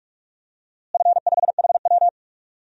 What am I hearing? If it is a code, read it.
Morse code, heard as a keyed tone, about 41 wpm, 700 Hz. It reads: F5HW
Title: DXCC-100 Call Signs - 4 Characters - Repeated 3x - 45wpm